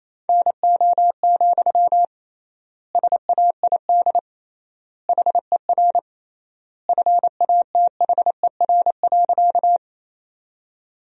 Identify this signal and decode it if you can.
Morse code, heard as a keyed tone, about 28 wpm, 700 Hz. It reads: NO, SAID HER FATHER.
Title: 3x Word-Spacing - Sentences from Top 300 Words - 28wpm